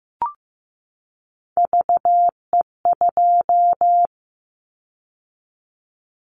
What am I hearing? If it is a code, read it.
Morse code, heard as a keyed tone, about 15 wpm, 700 Hz. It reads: VE2
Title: Callsign Prefixes for Top 50 DXCC - 15wpm